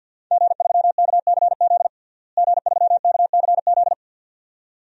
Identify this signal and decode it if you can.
Morse code, heard as a keyed tone, about 53 wpm, 700 Hz. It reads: G3KCZ G3KCZ
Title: DXCC-100 Call Signs - 5 Characters - Repeated 2x - 50wpm